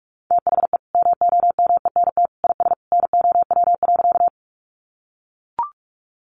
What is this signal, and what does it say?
Morse code, heard as a keyed tone, about 44 wpm, 700 Hz. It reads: THE MOMENT IS NOW.
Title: Sentences from Top 1000 Words 45wpm